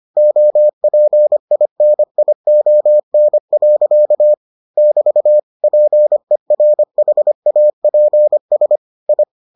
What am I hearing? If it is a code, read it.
Morse code, heard as a keyed tone, about 25 wpm, 600 Hz. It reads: OPINION. <BT> PERHAPS I